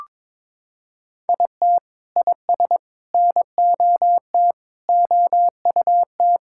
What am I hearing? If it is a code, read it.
Morse code, heard as a keyed tone, about 22 wpm, 700 Hz. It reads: IT IS NOT OUT
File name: Sentences with 9-Word Phrases - 22wpm